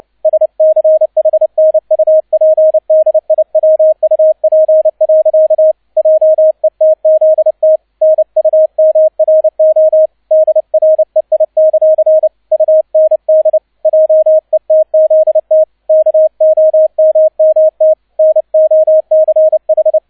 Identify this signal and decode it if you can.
Morse code, heard as a keyed tone, about 29 wpm, 630 Hz, with radio noise. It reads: SCHNUPDIWUP. JETZT NUMRO DREI; UND JETZT KOMMT NOCH